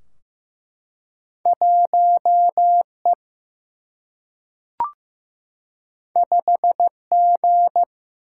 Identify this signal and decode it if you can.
Morse code, heard as a keyed tone, about 15 wpm, 700 Hz. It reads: 1E 5G
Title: Single-Letter Single-Number 15wpm